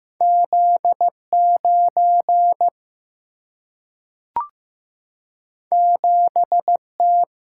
Morse code, heard as a keyed tone, about 15 wpm, 700 Hz. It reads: Z 9 7 T
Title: Single-Letter Single-Number 15wpm